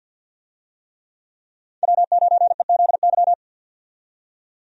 Morse code, heard as a keyed tone, about 51 wpm, 700 Hz. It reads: W9EZY
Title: USA Call Signs - 5 Characters - 50wpm